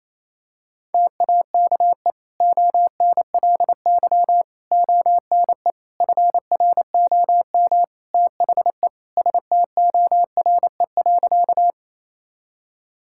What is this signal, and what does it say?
Morse code, heard as a keyed tone, about 28 wpm, 700 Hz. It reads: TAKE ONLY ONE FROM THE STORE.
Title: Sentences from Top 1000 Words - Review Part 5 - 28wpm